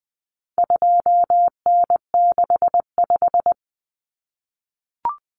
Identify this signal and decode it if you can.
Morse code, heard as a keyed tone, about 20 wpm, 700 Hz. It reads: 2N65
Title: Common Cut Numbers - 4-Digit 20wpm